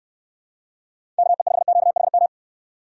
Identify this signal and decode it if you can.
Morse code, heard as a keyed tone, about 67 wpm, 700 Hz. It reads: KE5QHM